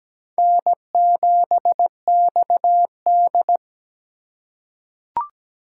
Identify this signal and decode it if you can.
Morse code, heard as a keyed tone, about 17 wpm, 700 Hz. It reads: N7XD
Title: USA Call Signs - 4x Character Spacing - 17wpm